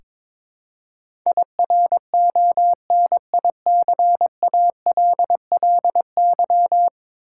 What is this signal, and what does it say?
Morse code, heard as a keyed tone, about 21 wpm, 700 Hz. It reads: IRONICALLY
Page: morse-code-ninja-practice-22wpm